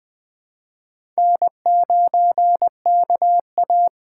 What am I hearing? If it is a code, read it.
Morse code, heard as a keyed tone, about 21 wpm, 700 Hz. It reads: N9KA
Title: USA Call Signs - 2x Character Spacing - 20wpm